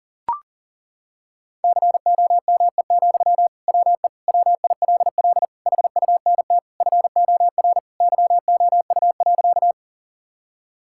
Morse code, heard as a keyed tone, about 40 wpm, 700 Hz. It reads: COME, WE WILL HUNT FOR YOU.
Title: Sentences from Top 1600 Words 40wpm